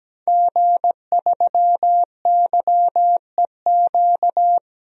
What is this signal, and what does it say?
Morse code, heard as a keyed tone, about 17 wpm, 700 Hz. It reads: G3YEQ
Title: DXCC-100 Call Signs - 5 Characters - 17wpm